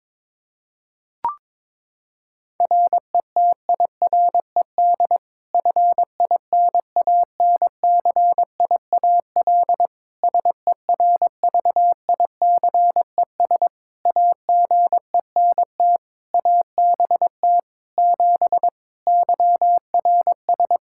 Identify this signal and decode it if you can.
Morse code, heard as a keyed tone, about 22 wpm, 700 Hz. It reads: RETIRED FINANCIAL SERVICES AGENT ABT 7 YRS
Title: K1USN SST QSO Practice 22wpm